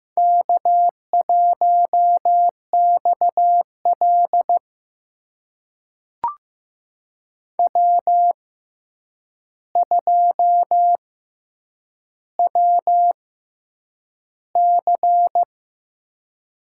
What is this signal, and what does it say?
Morse code, heard as a keyed tone, about 15 wpm, spaced as for 8 wpm, 700 Hz. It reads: K1XL W2WC
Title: USA Call Signs - 4 Characters - 6x Character Spacing - 15wpm